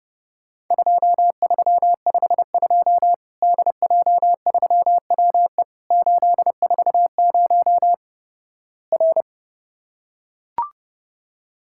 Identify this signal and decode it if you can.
Morse code, heard as a keyed tone, about 30 wpm, 700 Hz. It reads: 2352 DJ3WE 840 R
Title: Worked All Europe DX Contest - QTC Exchange - 30wpm